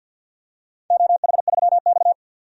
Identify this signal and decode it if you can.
Morse code, heard as a keyed tone, about 48 wpm, 700 Hz. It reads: OH3X
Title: DXCC-100 Call Signs - 4 Characters - 50wpm